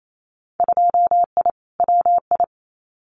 Morse code, heard as a keyed tone, about 28 wpm, 700 Hz. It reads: 2I WI
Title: Winter Field Day Exchanges 28wpm